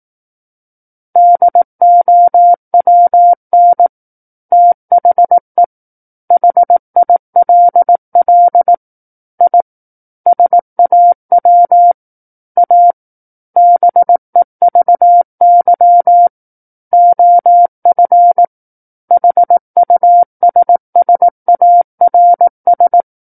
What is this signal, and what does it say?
Morse code, heard as a keyed tone, about 18 wpm, 700 Hz. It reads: DOWN THE HILL I SAW A BEVY OF HUSSARS
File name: 18 WPM